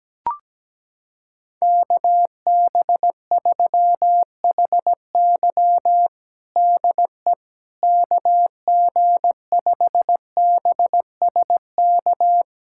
Morse code, heard as a keyed tone, about 17 wpm, 700 Hz. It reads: KB3HY DE KG5BSK
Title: USA Callsign DE Callsign 17wpm